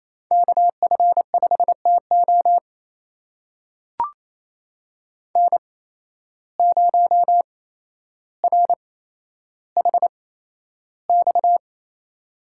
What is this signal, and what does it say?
Morse code, heard as a keyed tone, about 28 wpm, spaced as for 12 wpm, 700 Hz. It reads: KF5TO N0RHX
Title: USA Call Signs - 5 Characters - 8x Character Spacing - 28wpm